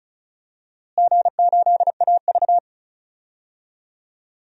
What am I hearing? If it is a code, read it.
Morse code, heard as a keyed tone, about 36 wpm, 700 Hz. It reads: G8AV
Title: DXCC-100 Call Signs - 4 Characters - Repeated 3x - 35wpm